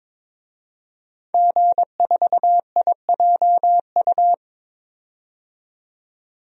Morse code, heard as a keyed tone, about 22 wpm, 700 Hz. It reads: G4IJU
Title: DXCC-100 Call Signs - 5 Characters - 22wpm